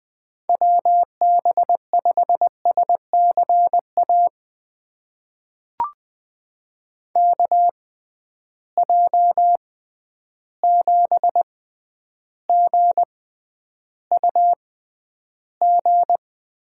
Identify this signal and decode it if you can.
Morse code, heard as a keyed tone, about 20 wpm, spaced as for 10 wpm, 700 Hz. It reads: WB5SCA KJ7GUG
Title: USA Call Signs - 6 Characters - 6x Character Spacing - 20wpm